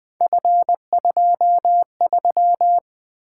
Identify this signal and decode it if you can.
Morse code, heard as a keyed tone, about 20 wpm, 700 Hz. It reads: F23
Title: Three Letter-Number Combinations - 20wpm